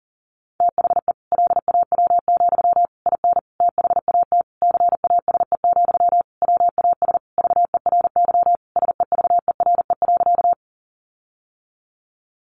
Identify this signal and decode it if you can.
Morse code, heard as a keyed tone, about 40 wpm, 700 Hz. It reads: THE LAW, IN THAT CASE, WAS VERY SEVERE.
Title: Sentences from Top 1400 Words 40wpm